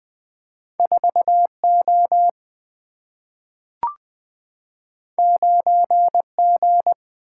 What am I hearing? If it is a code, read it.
Morse code, heard as a keyed tone, about 20 wpm, 700 Hz. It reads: 4 O 9 G
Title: Single-Letter Single-Number 20wpm